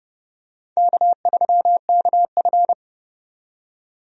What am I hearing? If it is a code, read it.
Morse code, heard as a keyed tone, about 30 wpm, 700 Hz. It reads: K3KF
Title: USA Call Signs - 4 Characters - 30wpm